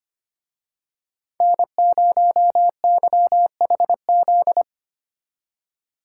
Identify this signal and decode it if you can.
Morse code, heard as a keyed tone, about 30 wpm, 700 Hz. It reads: N0YHZ
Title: Speed Racing - USA 1x3 Call Signs 25wpm